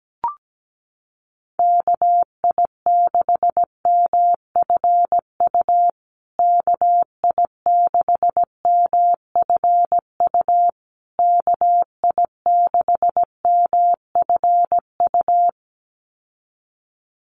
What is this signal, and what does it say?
Morse code, heard as a keyed tone, about 17 wpm, 700 Hz. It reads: KI6MFU KI6MFU KI6MFU
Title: USA Call Signs Repeated 3x 17wpm